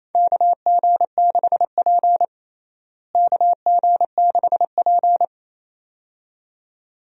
Morse code, heard as a keyed tone, about 28 wpm, 700 Hz. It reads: KG6P KG6P
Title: USA Call Signs - 4 Characters - Repeated 2x - 28wpm